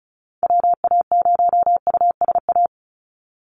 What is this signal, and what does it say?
Morse code, heard as a keyed tone, about 34 wpm, 700 Hz. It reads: WA0USA
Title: Call Signs Repeated 3x - 35wpm